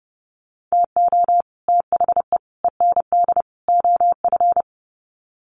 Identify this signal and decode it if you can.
Morse code, heard as a keyed tone, about 29 wpm, 700 Hz. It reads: TO THE END OF
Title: Sets of 4 Words 30wpm